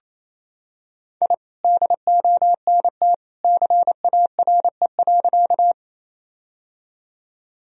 Morse code, heard as a keyed tone, about 28 wpm, 700 Hz. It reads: I DONT CARE.
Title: Sentences from Top 1000 Words - Review Part 1 - 28wpm